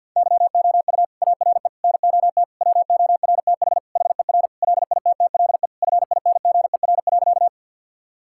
Morse code, heard as a keyed tone, about 50 wpm, 700 Hz. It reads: YOU ARE NOT WORTH HER LITTLE FINGER.
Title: Sentences from Top 1600 Words 50wpm